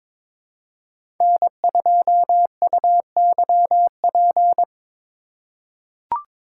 Morse code, heard as a keyed tone, about 22 wpm, 700 Hz. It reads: N2UYP
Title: USA Call Signs - 5 Characters - 4x Character Spacing - 22wpm